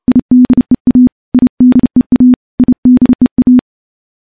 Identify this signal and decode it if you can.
Morse code, heard as a keyed tone, about 30 wpm, 250 Hz. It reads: IDEA IDEA IDEA
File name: morse / morse-training-30WPM